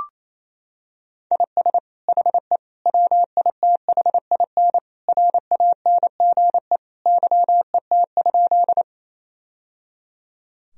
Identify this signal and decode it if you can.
Morse code, heard as a keyed tone, about 28 wpm, 700 Hz. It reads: IS HE WITHIN RANGE YET?
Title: Sentences from Top 1000 Words - Review Part 4 - 28wpm